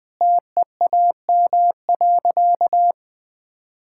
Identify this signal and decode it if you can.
Morse code, heard as a keyed tone, about 20 wpm, 700 Hz. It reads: TEAM.
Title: Sentences with 4-Word Phrases - Spoken Beginning - 20wpm